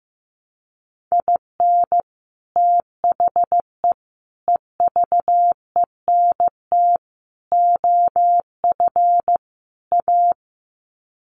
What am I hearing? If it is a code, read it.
Morse code, heard as a keyed tone, about 15 wpm, 700 Hz. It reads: IN THE EVENT OF A
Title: Sets of 5 Words Encore 15wpm